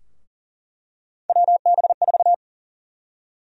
Morse code, heard as a keyed tone, about 40 wpm, 700 Hz. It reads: WB4
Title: USA Call Sign Prefixes - Repeated 2x - 40wpm